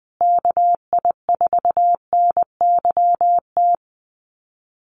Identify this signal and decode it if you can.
Morse code, heard as a keyed tone, about 20 wpm, 700 Hz. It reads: KI4NYT
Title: USA Call Signs Repeated 2x 20wpm